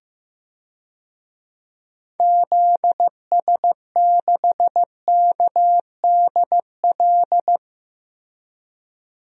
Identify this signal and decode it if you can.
Morse code, heard as a keyed tone, about 16 wpm, 700 Hz. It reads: ZS6KDL
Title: DXCC-100 Call Signs - 6 Characters - Repeated 2x - 15wpm